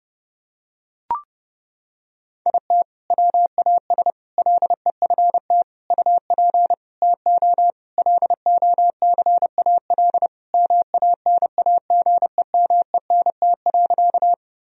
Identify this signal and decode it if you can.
Morse code, heard as a keyed tone, about 30 wpm, 700 Hz. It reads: IT WAS LEFT UP TO LOCAL MANAGEMENT.
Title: Sentences from Top 1000 Words - Review Part 4 - 30wpm